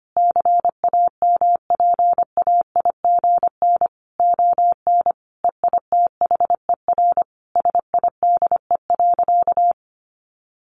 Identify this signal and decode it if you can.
Morse code, heard as a keyed tone, about 25 wpm, 700 Hz. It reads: CAMPAIGN ON EITHER SIDE.
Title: Sentences from Top 1200 Words 25wpm